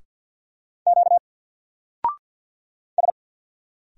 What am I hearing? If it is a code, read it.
Morse code, heard as a keyed tone, about 48 wpm, 700 Hz. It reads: Q S
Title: Single Character (Letter:Number:Punctuation:Prosign) Alt 50wpm